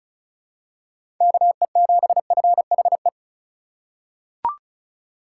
Morse code, heard as a keyed tone, about 36 wpm, 700 Hz. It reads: KE7FHE
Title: USA Call Signs - 6 Characters - 3x Character Spacing - 35wpm